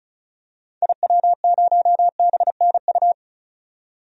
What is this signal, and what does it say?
Morse code, heard as a keyed tone, about 34 wpm, 700 Hz. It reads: IW0BNU
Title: DXCC-100 Call Signs - 6 Characters - 35wpm